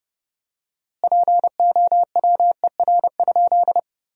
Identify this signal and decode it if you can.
Morse code, heard as a keyed tone, about 31 wpm, 700 Hz. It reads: POWER?
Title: 4x Word-Spacing - Sentences from Top 500 Words - 30wpm